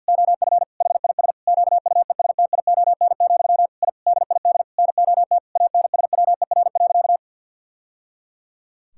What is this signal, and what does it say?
Morse code, heard as a keyed tone, about 46 wpm, 700 Hz. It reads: OF HIS QUESTION, I DID NOT ANSWER.
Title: Sentences from Top 1000 Words - Review Part 6 - 50wpm